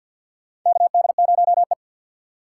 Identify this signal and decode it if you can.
Morse code, heard as a keyed tone, about 47 wpm, 700 Hz. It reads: KD0E
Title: USA Call Signs - 4x Character Spacing - 50wpm